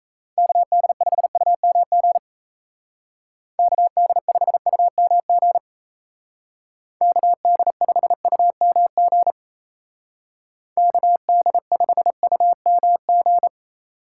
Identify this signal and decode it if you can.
Morse code, heard as a keyed tone, about 34 wpm, 700 Hz. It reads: KD5UMG KD5UMG KD5UMG KD5UMG
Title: Speed Racing - USA Call Signs 28wpm